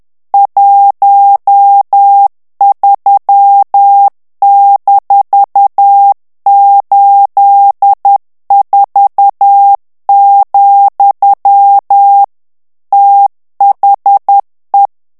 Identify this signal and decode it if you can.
Morse code, heard as a keyed tone, about 10 wpm, 800 Hz. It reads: 13-84, THE
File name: ANGLAIS texte clair 3 vitesse 10 mots minute